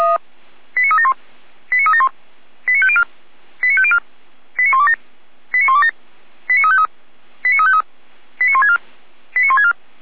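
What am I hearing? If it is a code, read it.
Five-tone selective call (ZVEI), heard as quick runs of five tones, tones of about 70 ms. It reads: 89371 89371 89504 89504 89127 89127 89353 89353 89164 89164
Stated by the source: ZVEI five-tone selective call 89235 received on SDR, from the Feuerwehr Orlamünde fire brigade audio server.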